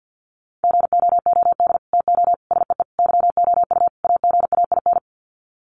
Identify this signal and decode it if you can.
Morse code, heard as a keyed tone, about 48 wpm, 700 Hz. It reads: GOOD TO SEE YOU AGAIN
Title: Sets of 5 Words Encore2 50wpm